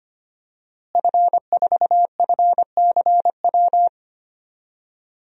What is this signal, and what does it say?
Morse code, heard as a keyed tone, about 25 wpm, 700 Hz. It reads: F4FCW
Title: DXCC-100 Call Signs - 5 Characters - Repeated 3x - 25wpm